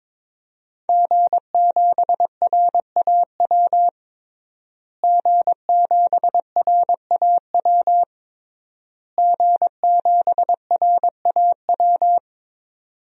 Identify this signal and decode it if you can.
Morse code, heard as a keyed tone, about 22 wpm, 700 Hz. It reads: G7RAW G7RAW G7RAW
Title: DXCC-100 Call Signs - 5 Characters - Repeated 3x - 22wpm